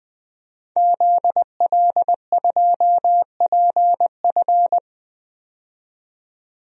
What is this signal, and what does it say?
Morse code, heard as a keyed tone, about 21 wpm, 700 Hz. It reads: ZL2PF
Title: DXCC-100 Call Signs - 5 Characters - 20wpm